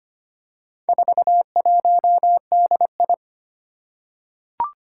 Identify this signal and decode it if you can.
Morse code, heard as a keyed tone, about 25 wpm, 700 Hz. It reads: 41DI
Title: Four Challenging Letter-Number Combinations 25wpm